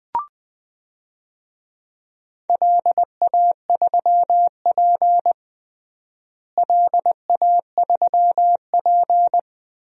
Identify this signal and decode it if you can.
Morse code, heard as a keyed tone, about 20 wpm, 700 Hz. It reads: LA3P LA3P
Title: DXCC-100 Call Signs - 4 Characters - Repeated 3x - 20wpm